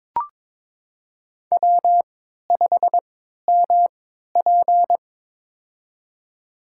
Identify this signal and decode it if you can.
Morse code, heard as a keyed tone, about 22 wpm, spaced as for 13 wpm, 700 Hz. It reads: W5MP
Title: USA Call Signs - 4 Characters - 3x Character Spacing - 22wpm